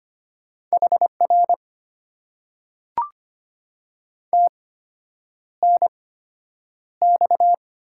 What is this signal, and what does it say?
Morse code, heard as a keyed tone, about 25 wpm, spaced as for 9 wpm, 700 Hz. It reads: HR TNX
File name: Top 90 QSO Elements - 8x Character Spacing - 25wpm